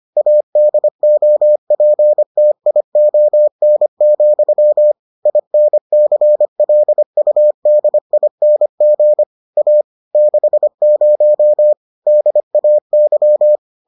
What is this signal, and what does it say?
Morse code, heard as a keyed tone, about 25 wpm, 600 Hz. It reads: ADOPTION, INCLUDING A 60 DAY